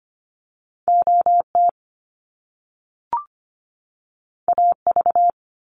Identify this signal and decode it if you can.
Morse code, heard as a keyed tone, about 25 wpm, 700 Hz. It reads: OT AV
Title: Top Two Letter Combinations in Words 25wpm